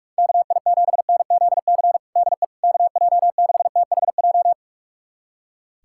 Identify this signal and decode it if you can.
Morse code, heard as a keyed tone, about 41 wpm, 700 Hz. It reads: KI7NZC DE KJ6THJ
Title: USA Callsign DE Callsign - 6 Characters - 45wpm